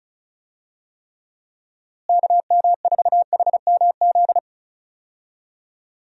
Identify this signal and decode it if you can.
Morse code, heard as a keyed tone, about 34 wpm, 700 Hz. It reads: KM4HMZ
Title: USA Call Signs - 6 Characters - Repeated 3x - 35wpm